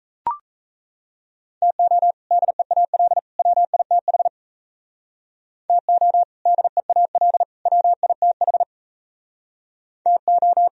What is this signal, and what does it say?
Morse code, heard as a keyed tone, about 35 wpm, 700 Hz. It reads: TO DEAL WITH TO DEAL WITH TO